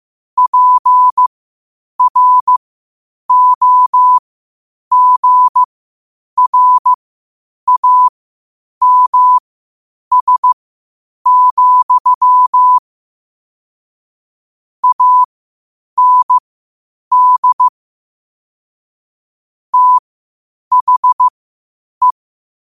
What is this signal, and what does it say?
Morse code, heard as a keyed tone, about 15 wpm, spaced as for 8 wpm, 1000 Hz. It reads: PROGRAMS, AND THE
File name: qotdfiles / 8wpm